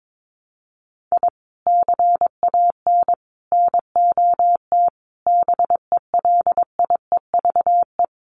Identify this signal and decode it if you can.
Morse code, heard as a keyed tone, about 22 wpm, 700 Hz. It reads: I CAN NOT BELIEVE
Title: Sets of 5 Words Encore 22wpm